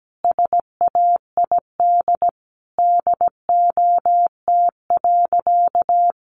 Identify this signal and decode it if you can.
Morse code, heard as a keyed tone, about 17 wpm, 700 Hz. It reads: SAID DOT.
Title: Sentences from Top 1500 Words 17wpm